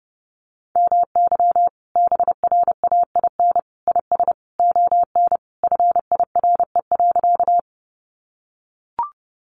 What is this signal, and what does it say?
Morse code, heard as a keyed tone, about 30 wpm, 700 Hz. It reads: MY BRAIN IS ON FIRE.
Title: Sentences from Top 1100 Words 30wpm